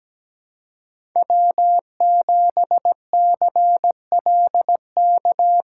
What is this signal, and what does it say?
Morse code, heard as a keyed tone, about 20 wpm, 700 Hz. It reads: W7CLK
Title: Speed Racing - USA 1x3 Call Signs 17wpm